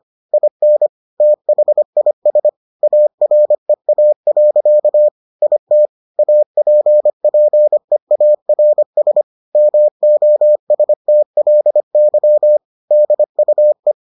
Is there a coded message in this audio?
Yes: Morse code from 0.3 to 13.9 s, heard as a keyed tone, about 25 wpm, 600 Hz. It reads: IN THIS AREA. IT APPEARS MOSTLY DUE